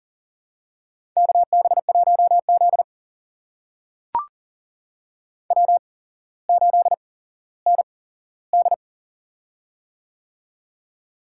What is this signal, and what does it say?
Morse code, heard as a keyed tone, about 40 wpm, spaced as for 17 wpm, 700 Hz. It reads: KB1Z W8ND
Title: USA Call Signs - 4 Characters - 8x Character Spacing - 40wpm